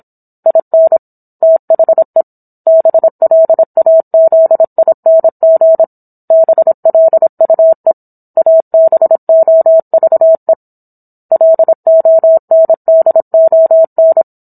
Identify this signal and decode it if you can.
Morse code, heard as a keyed tone, about 27 wpm, 650 Hz. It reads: IN THE BLAZING BLUE ABOVE LONDON